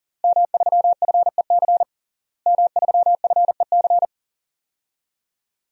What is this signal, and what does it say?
Morse code, heard as a keyed tone, about 37 wpm, 700 Hz. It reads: M3FEC M3FEC
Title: DXCC-100 Call Signs - 5 Characters - Repeated 2x - 40wpm